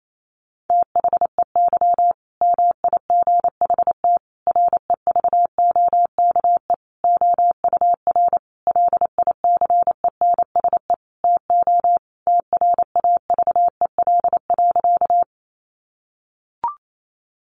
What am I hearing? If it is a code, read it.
Morse code, heard as a keyed tone, about 28 wpm, 700 Hz. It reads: THEY MIGHT REVOKE OUR LICENSE TO TRAVEL.
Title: A Sound Of Thunder 28wpm